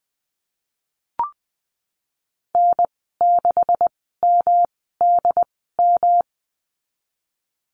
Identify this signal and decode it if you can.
Morse code, heard as a keyed tone, about 19 wpm, spaced as for 14 wpm, 700 Hz. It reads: N6MDM
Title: 128 Lesson 21g - Number 6 - Call Signs 20wpm